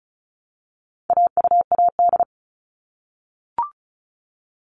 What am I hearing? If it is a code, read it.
Morse code, heard as a keyed tone, about 34 wpm, 700 Hz. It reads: AUAD
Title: All Cut Numbers - 4-Digit 35wpm